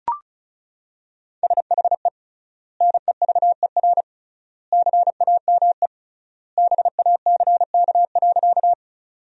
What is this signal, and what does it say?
Morse code, heard as a keyed tone, about 36 wpm, 700 Hz. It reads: SHE NEVER CAME BACK.
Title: 3x Word-Spacing - Sentences from Top 300 Words - 35wpm